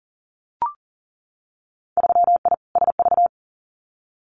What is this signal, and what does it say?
Morse code, heard as a keyed tone, about 38 wpm, 700 Hz. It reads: 3I SV
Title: Winter Field Day Exchanges Encore 40wpm